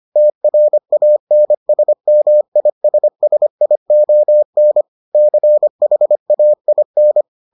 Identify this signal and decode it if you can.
Morse code, heard as a keyed tone, about 25 wpm, 600 Hz. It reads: TRANSMISSION CHAIN